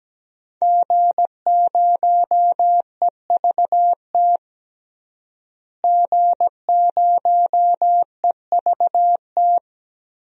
Morse code, heard as a keyed tone, about 17 wpm, 700 Hz. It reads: G0EVT G0EVT
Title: DXCC-100 Call Signs - 5 Characters - Repeated 3x - 17wpm